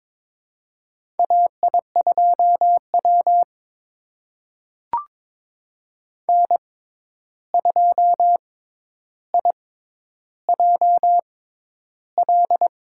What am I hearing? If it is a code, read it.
Morse code, heard as a keyed tone, about 21 wpm, spaced as for 10 wpm, 700 Hz. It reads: AI2W N2IJL
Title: USA Call Signs - 6x Character Spacing - 22wpm